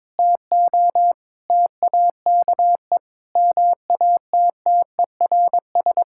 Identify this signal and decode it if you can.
Morse code, heard as a keyed tone, about 22 wpm, 700 Hz. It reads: TO TAKE MATTERS
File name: Sentences with 7-Word Phrases - 22wpm